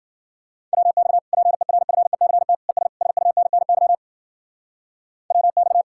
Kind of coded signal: Morse code

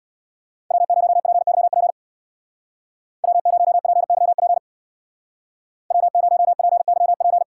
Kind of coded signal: Morse code

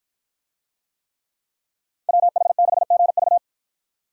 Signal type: Morse code

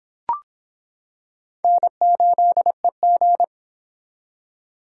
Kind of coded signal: Morse code

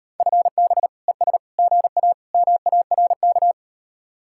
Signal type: Morse code

